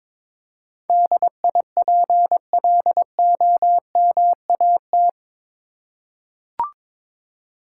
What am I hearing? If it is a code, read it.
Morse code, heard as a keyed tone, about 22 wpm, 700 Hz. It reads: DIPLOMAT
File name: practice / 22wpm / Eight Letter Words - 22wpm